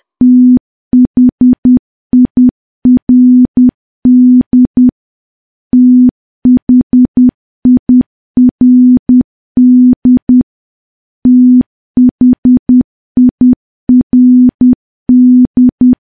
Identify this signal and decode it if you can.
Morse code, heard as a keyed tone, about 10 wpm, 250 Hz. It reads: THIRD THIRD THIRD
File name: morse / morse-training-10WPM